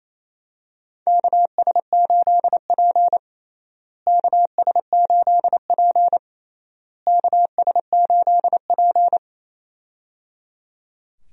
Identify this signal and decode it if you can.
Morse code, heard as a keyed tone, about 28 wpm, 700 Hz. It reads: KS8P KS8P KS8P
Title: USA Call Signs - 4 Characters - Repeated 3x - 28wpm